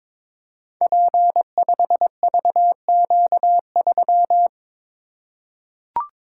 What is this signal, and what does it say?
Morse code, heard as a keyed tone, about 22 wpm, 690 Hz. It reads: P5VQ3
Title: Code Groups - 5 Letter:Numbers - 22wpm